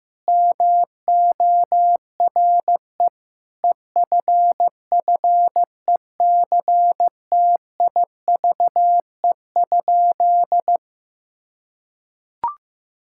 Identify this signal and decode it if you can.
Morse code, heard as a keyed tone, about 15 wpm, 700 Hz. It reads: MORE EFFECTIVE?
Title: Sentences from Top 1000 Words - Review Part 6 - 15wpm